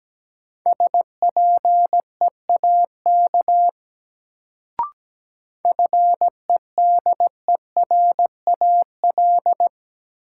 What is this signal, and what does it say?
Morse code, heard as a keyed tone, about 17 wpm, 700 Hz. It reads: SPEAK FEDERAL
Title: Sentences from Top 1100 Words - New Word Drill Alt - 17wpm